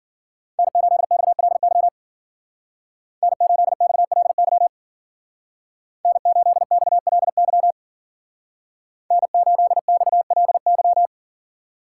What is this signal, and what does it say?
Morse code, heard as a keyed tone, about 49 wpm, 700 Hz. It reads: N8XLY N8XLY N8XLY N8XLY
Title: Speed Racing - USA 1x3 Call Signs 40wpm